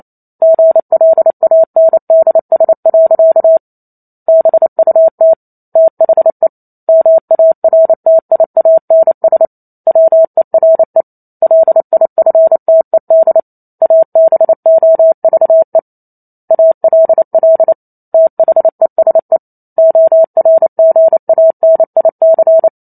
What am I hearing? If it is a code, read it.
Morse code, heard as a keyed tone, about 28 wpm, 650 Hz. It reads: GLANDS. BUT THE MARTIANS WERE LIFTED ABOVE ALL THESE ORGANIC